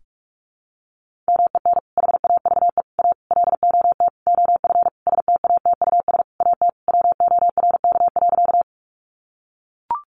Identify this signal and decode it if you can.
Morse code, heard as a keyed tone, about 44 wpm, 700 Hz. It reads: MEN HAVE A LOT OF STATUS AT WORK.
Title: Sentences from Top 1400 Words 45wpm